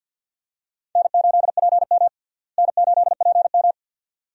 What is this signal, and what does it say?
Morse code, heard as a keyed tone, about 47 wpm, 690 Hz. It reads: N8PM N8PM
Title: USA Call Signs - 4 Characters - Repeated 2x - 50wpm